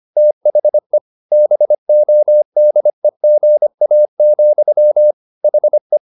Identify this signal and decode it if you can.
Morse code, heard as a keyed tone, about 25 wpm, 600 Hz. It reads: THE BODEGA, HE
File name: the old man and the sea part 1 25wpm